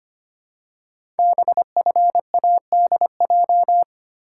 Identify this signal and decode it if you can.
Morse code, heard as a keyed tone, about 25 wpm, 700 Hz. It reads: BFADJ
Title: Code Groups - 5 Letters - 25wpm